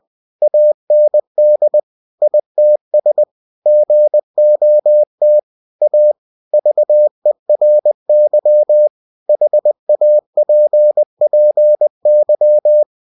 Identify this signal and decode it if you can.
Morse code, heard as a keyed tone, about 20 wpm, 600 Hz. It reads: AND ITS GOT A VERY HAPPY
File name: cw-books / winnie-the-pooh / winnie-20wpm-page-0057